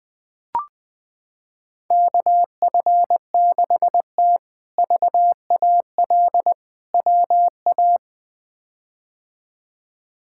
Morse code, heard as a keyed tone, about 21 wpm, 700 Hz. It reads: KF6T VAL WA
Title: K1USN SST QSO Practice - 20wpm